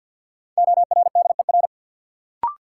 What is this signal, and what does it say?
Morse code, heard as a keyed tone, about 53 wpm, 700 Hz. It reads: ORDER
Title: Sets of 2 Words - New Word Drill - 50wpm